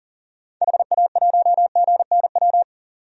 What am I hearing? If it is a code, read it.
Morse code, heard as a keyed tone, about 40 wpm, 700 Hz. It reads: HA1GNW
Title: DXCC-100 Call Signs - 6 Characters - 40wpm